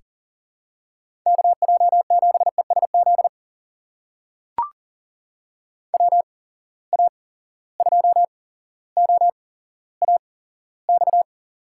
Morse code, heard as a keyed tone, about 40 wpm, spaced as for 17 wpm, 700 Hz. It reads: KJ7ESZ WA2OAX
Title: USA Call Signs - 6 Characters - 8x Character Spacing - 40wpm